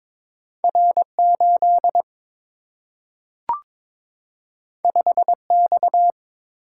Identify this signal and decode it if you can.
Morse code, heard as a keyed tone, about 21 wpm, 700 Hz. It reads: R 8 5 X
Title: Single-Letter Single-Number 22wpm